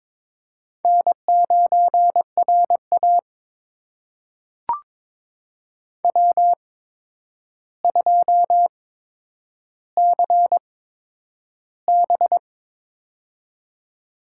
Morse code, heard as a keyed tone, about 22 wpm, spaced as for 9 wpm, 700 Hz. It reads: N9RA W2CB
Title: USA Call Signs - 8x Character Spacing - 22wpm